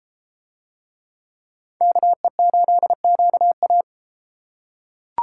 Morse code, heard as a keyed tone, about 32 wpm, 700 Hz. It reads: KE8QA